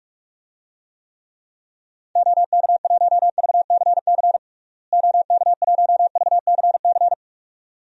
Morse code, heard as a keyed tone, about 46 wpm, 700 Hz. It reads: OK1VCC OK1VCC
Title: DXCC-100 Call Signs - 6 Characters - Repeated 2x - 45wpm